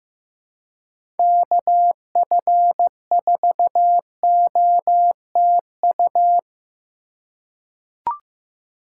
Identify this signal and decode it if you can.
Morse code, heard as a keyed tone, about 15 wpm, 700 Hz. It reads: KF4OTU
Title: USA Call Signs - 6 Characters - 2x Character Spacing - 15wpm